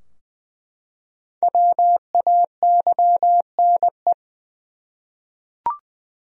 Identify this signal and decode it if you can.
Morse code, heard as a keyed tone, about 20 wpm, 700 Hz. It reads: WAYNE
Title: USA Common Names - 20wpm